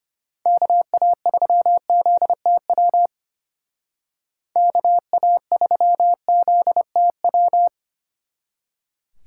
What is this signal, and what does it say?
Morse code, heard as a keyed tone, about 30 wpm, 700 Hz. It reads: KA3ZTW KA3ZTW
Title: Speed Racing - USA 2x3 Call Signs 25wpm